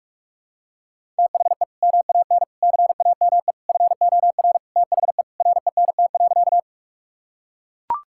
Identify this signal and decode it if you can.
Morse code, heard as a keyed tone, about 42 wpm, 700 Hz. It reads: THE MAN CAME FOR THE RENT.
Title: Sentences from Top 1000 Words - Review Part 5 - 45wpm